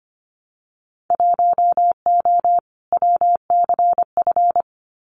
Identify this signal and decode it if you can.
Morse code, heard as a keyed tone, about 25 wpm, 700 Hz. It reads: 1O WCF
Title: Winter Field Day Exchanges 25wpm